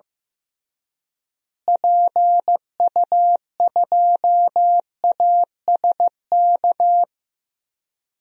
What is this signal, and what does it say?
Morse code, heard as a keyed tone, about 15 wpm, 700 Hz. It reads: PU2ASK
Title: DXCC-100 Call Signs - 6 Characters - Repeated 3x - 15wpm